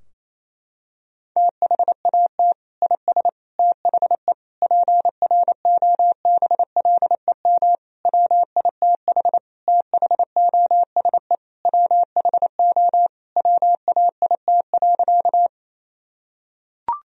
Morse code, heard as a keyed tone, about 28 wpm, 700 Hz. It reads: THAT IS THE PROBLEM WITH THOSE WHO WAIT.
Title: Sentences from Top 1000 Words - Review Part 2 - 28wpm